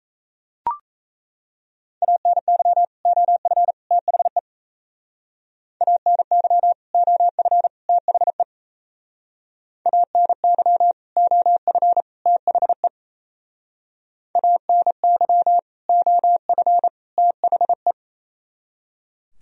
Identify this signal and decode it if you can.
Morse code, heard as a keyed tone, about 34 wpm, 700 Hz. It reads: ANY OF THE ANY OF THE ANY OF THE ANY OF THE
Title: Speed-Racing - Sets of 3 Words 28wpm